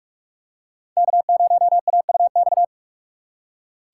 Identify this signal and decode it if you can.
Morse code, heard as a keyed tone, about 44 wpm, 700 Hz. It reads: K0AUX